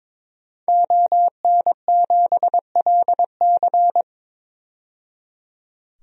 Morse code, heard as a keyed tone, about 22 wpm, 700 Hz. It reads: ON7LC
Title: DXCC-100 Call Signs - 5 Characters - 22wpm